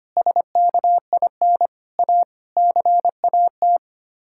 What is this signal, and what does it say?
Morse code, heard as a keyed tone, about 25 wpm, 700 Hz. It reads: SKIN A CAT
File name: Sets of 9 Words - 25wpm